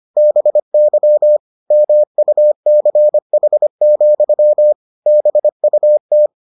Morse code, heard as a keyed tone, about 25 wpm, 600 Hz. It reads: BY MUCH, BUT